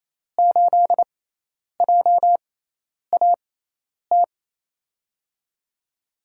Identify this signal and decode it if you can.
Morse code, heard as a keyed tone, about 28 wpm, spaced as for 10 wpm, 700 Hz. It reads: 8JAT
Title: USA Call Signs - 6 Characters - 6x Character Spacing - 28wpm